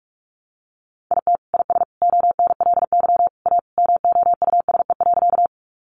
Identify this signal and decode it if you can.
Morse code, heard as a keyed tone, about 44 wpm, 700 Hz. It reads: IT IS ONLY A MOUSE.
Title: Sentences from Top 700 Words 45wpm